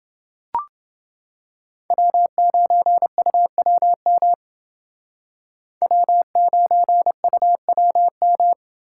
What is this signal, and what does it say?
Morse code, heard as a keyed tone, about 28 wpm, 700 Hz. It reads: W9UWM W9UWM